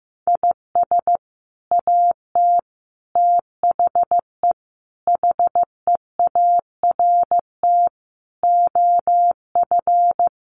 Morse code, heard as a keyed tone, about 15 wpm, 700 Hz. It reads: IS AT THE HEART OF
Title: Sets of 5 Words Encore2 15wpm